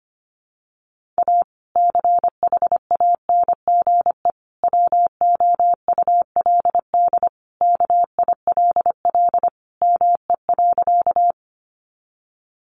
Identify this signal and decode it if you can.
Morse code, heard as a keyed tone, about 25 wpm, 700 Hz. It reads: A CHANGE WOULD KILL ME.
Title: Sentences from Top 1400 Words 25wpm